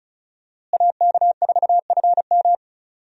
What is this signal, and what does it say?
Morse code, heard as a keyed tone, about 36 wpm, 700 Hz. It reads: AK4FM
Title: USA Call Signs - 5 Characters - Repeated 2x - 35wpm